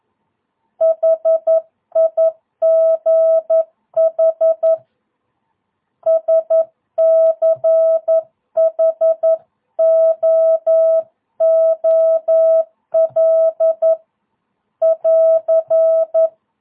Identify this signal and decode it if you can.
Morse code, heard as a keyed tone, about 11 wpm, 660 Hz. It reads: HIGH SCHOOL <AR>
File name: jh8cbh / 2022cw